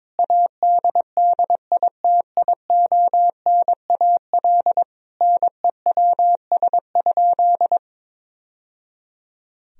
Morse code, heard as a keyed tone, about 22 wpm, 700 Hz. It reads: ADDITIONAL NEWS?
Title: Sentences from Top 1000 Words - Review Part 4 - 22wpm